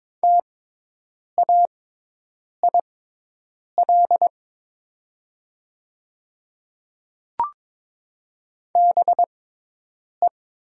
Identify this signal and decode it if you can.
Morse code, heard as a keyed tone, about 22 wpm, spaced as for 8 wpm, 700 Hz. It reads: TAIL BE
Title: 6x Character Spacing - Top 500 Words - 22wpm